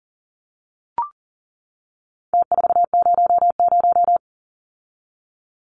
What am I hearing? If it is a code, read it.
Morse code, heard as a keyed tone, about 40 wpm, 700 Hz. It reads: T400
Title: Numbers - Contest Style - 4-Digit 40wpm